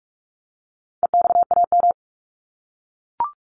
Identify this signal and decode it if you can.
Morse code, heard as a keyed tone, about 44 wpm, 700 Hz. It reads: EXAM
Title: Top 501-750 Words 45wpm